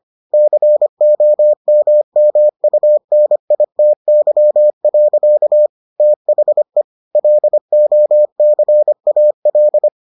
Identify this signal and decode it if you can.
Morse code, heard as a keyed tone, about 25 wpm, 600 Hz. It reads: COMMUNITY. THE LOCAL